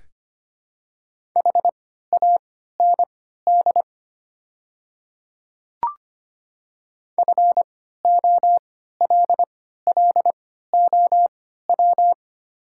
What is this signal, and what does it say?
Morse code, heard as a keyed tone, about 25 wpm, spaced as for 14 wpm, 700 Hz. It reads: HAND FOLLOW
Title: 3x Character Spacing - Top 200 Words - 25wpm